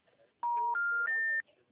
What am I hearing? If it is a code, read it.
Special information tone.